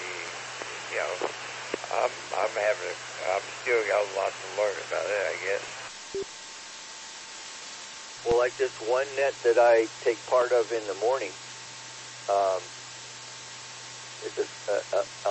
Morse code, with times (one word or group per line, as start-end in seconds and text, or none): none